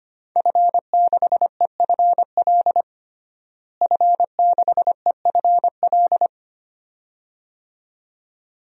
Morse code, heard as a keyed tone, about 25 wpm, 700 Hz. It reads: F6EFL F6EFL
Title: DXCC-100 Call Signs - 5 Characters - Repeated 2x - 25wpm